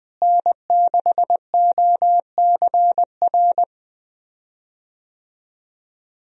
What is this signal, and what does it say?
Morse code, heard as a keyed tone, about 20 wpm, 700 Hz. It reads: N6OCR
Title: USA Call Signs - 5 Characters - Repeated 2x - 20wpm